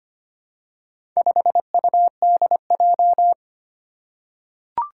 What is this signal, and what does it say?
Morse code, heard as a keyed tone, about 25 wpm, 700 Hz. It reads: 5UDJ
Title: Four Challenging Letter-Number Combinations 25wpm